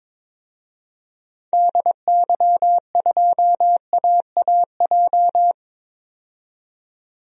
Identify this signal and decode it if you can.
Morse code, heard as a keyed tone, about 22 wpm, 700 Hz. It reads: DY2AAJ
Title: DXCC-100 Call Signs - 6 Characters - 22wpm